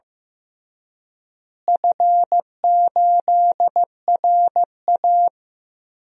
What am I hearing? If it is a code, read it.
Morse code, heard as a keyed tone, about 15 wpm, 700 Hz. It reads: F8RA
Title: DXCC-100 Call Signs - 4 Characters - Repeated 3x - 15wpm